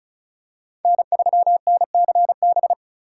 Morse code, heard as a keyed tone, about 36 wpm, 690 Hz. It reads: N3NCB
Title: USA Call Signs - 5 Characters - 3x Character Spacing - 35wpm